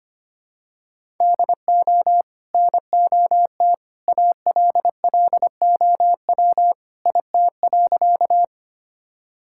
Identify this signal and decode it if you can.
Morse code, heard as a keyed tone, about 25 wpm, 700 Hz. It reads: DO NOT ALLOW IT.
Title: Sentences from Top 1000 Words - Review Part 5 - 25wpm